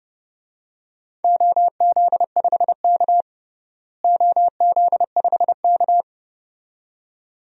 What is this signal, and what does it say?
Morse code, heard as a keyed tone, about 30 wpm, 700 Hz. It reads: OZ5K OZ5K
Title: DXCC-100 Call Signs - 4 Characters - Repeated 2x - 30wpm